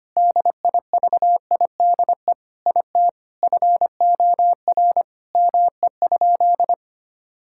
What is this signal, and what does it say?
Morse code, heard as a keyed tone, about 25 wpm, 700 Hz. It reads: DIVIDE IT FOR ME?
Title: Sentences from Top 1600 Words 25wpm